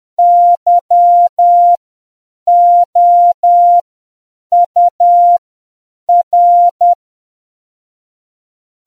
Morse code, heard as a keyed tone, about 10 wpm, spaced as for 7 wpm, 700 Hz. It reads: YOUR